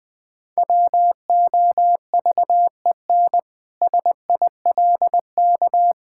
Morse code, heard as a keyed tone, about 20 wpm, 700 Hz. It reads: WOVEN SILK
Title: Pangram Challenge v2 - 20wpm